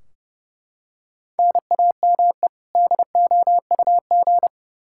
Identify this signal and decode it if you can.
Morse code, heard as a keyed tone, about 30 wpm, 700 Hz. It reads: NAME DOUG
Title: Practice QSO - Exchange 1 - Beginning - 30wpm